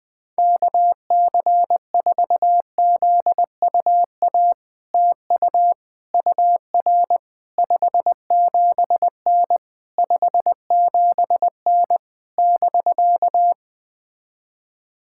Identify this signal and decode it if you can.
Morse code, heard as a keyed tone, about 21 wpm, 700 Hz. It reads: KC4ZUA TU UR 57N 57N <BK>
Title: POTA QSO Practice - 20wpm